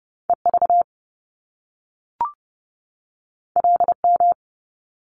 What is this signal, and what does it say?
Morse code, heard as a keyed tone, about 30 wpm, 700 Hz. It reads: EV LM
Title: Top Two Letter Combinations in Words 30wpm